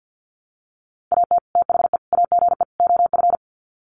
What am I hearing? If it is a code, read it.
Morse code, heard as a keyed tone, about 50 wpm, 700 Hz. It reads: AT THE AGE OF
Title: Sets of 4 Words 50wpm